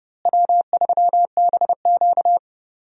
Morse code, heard as a keyed tone, about 30 wpm, 700 Hz. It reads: W3BQ
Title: USA Call Signs - 4 Characters - Repeated 2x - 30wpm